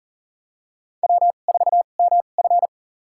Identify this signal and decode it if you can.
Morse code, heard as a keyed tone, about 39 wpm, spaced as for 29 wpm, 700 Hz. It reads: W4MF
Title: USA Call Signs - 4 Characters - 2x Character Spacing - 40wpm